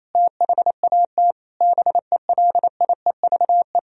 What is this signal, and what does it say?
Morse code, heard as a keyed tone, about 28 wpm, 700 Hz. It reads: THAT BELIEVE
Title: Sentences from Top 1000 Words - Review Part 1 - 28wpm